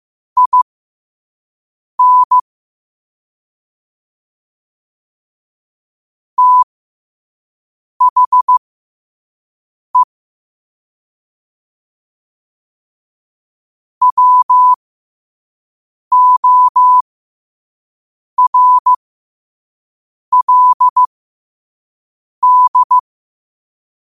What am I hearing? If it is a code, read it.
Morse code, heard as a keyed tone, about 15 wpm, spaced as for 5 wpm, 1000 Hz. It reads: IN THE WORLD